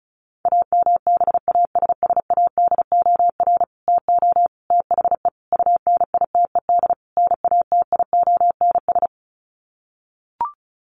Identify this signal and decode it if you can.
Morse code, heard as a keyed tone, about 34 wpm, 700 Hz. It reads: AMBASSADOR TO THE UNITED NATIONS
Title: Sets of 5 Words Encore2 35wpm